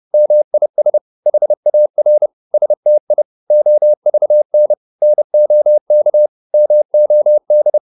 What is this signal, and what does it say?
Morse code, heard as a keyed tone, about 30 wpm, 600 Hz. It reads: MIS HAR STI OVN NOK MOD
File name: Korte ord DK 30wpm